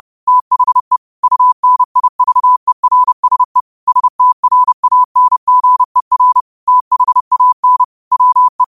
Morse code, heard as a keyed tone, about 30 wpm, 1000 Hz. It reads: THE UNIVERSE STRANGER THAN WE
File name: 30wpm